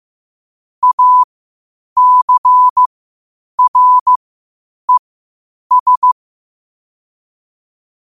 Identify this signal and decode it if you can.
Morse code, heard as a keyed tone, about 15 wpm, spaced as for 8 wpm, 1000 Hz. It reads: ACRES
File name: qotdfiles / 8wpm